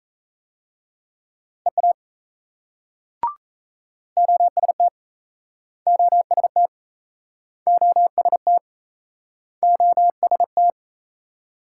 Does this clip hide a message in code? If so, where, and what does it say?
Yes: Morse code from 1.7 to 10.7 s, heard as a keyed tone, about 38 wpm, 700 Hz. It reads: EA OST OST OST OST